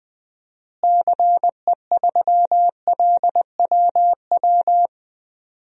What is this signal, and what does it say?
Morse code, heard as a keyed tone, about 20 wpm, 700 Hz. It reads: CE3LWW
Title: DXCC-100 Call Signs - 6 Characters - Repeated 3x - 20wpm